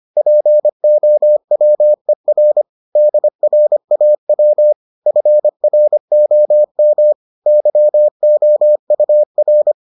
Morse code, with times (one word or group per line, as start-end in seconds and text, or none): 0.2-2.6 POWER
3.0-4.7 DRAW
5.1-7.1 FROM
7.5-9.7 YOUR